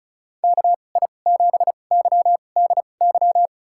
Morse code, heard as a keyed tone, about 36 wpm, spaced as for 25 wpm, 700 Hz. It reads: KI7YDY
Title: USA Call Signs - 6 Characters - 2x Character Spacing - 35wpm